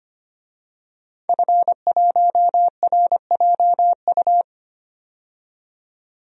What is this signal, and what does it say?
Morse code, heard as a keyed tone, about 25 wpm, 700 Hz. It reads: F1RJU
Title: DXCC-100 Call Signs - 5 Characters - 25wpm